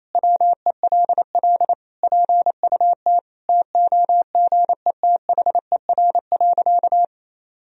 Morse code, heard as a keyed tone, about 28 wpm, 700 Hz. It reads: WELL PUT TOGETHER.
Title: Sentences from Top 1000 Words - Review Part 3 - 28wpm